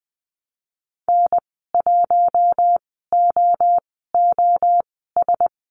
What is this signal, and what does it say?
Morse code, heard as a keyed tone, about 20 wpm, spaced as for 15 wpm, 700 Hz. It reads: N1OOS
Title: 042 Lesson 8g - Number 1 - Call Signs 20wpm